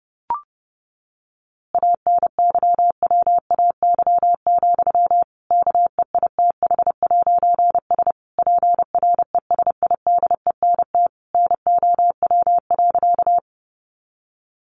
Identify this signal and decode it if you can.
Morse code, heard as a keyed tone, about 30 wpm, 700 Hz. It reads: ANYWAY, KEITH'S PRESIDENT NOW.
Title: A Sound Of Thunder 30wpm